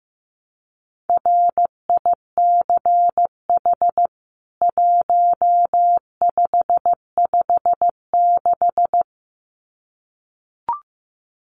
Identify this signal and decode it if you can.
Morse code, heard as a keyed tone, about 15 wpm, 700 Hz. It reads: RICH 1556
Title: CWT Exchange 15wpm